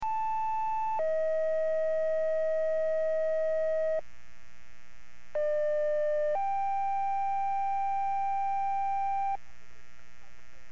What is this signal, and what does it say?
Two-tone page, heard as a one-second tone, then a three-second tone, about 880 then 640 Hz, about 600 then 790 Hz. It is heard from 0.0 to 9.4 s.